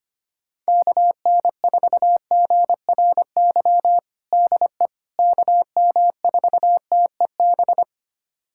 Morse code, heard as a keyed tone, about 25 wpm, 700 Hz. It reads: KN4GRY DE KM4TEB
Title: USA Callsign DE Callsign 25wpm